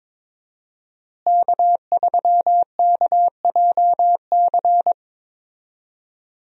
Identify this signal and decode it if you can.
Morse code, heard as a keyed tone, about 22 wpm, 700 Hz. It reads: K3KJC
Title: USA Call Signs - 5 Characters - Repeated 2x - 22wpm